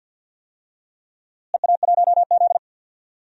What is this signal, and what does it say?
Morse code, heard as a keyed tone, about 53 wpm, 700 Hz. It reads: EA1Z
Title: DXCC-100 Call Signs - 4 Characters - Repeated 3x - 50wpm